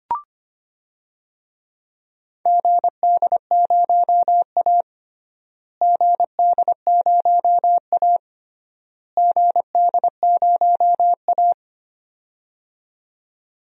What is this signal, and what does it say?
Morse code, heard as a keyed tone, about 25 wpm, 700 Hz. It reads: GD0A GD0A GD0A
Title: DXCC-100 Call Signs - 4 Characters - Repeated 3x - 25wpm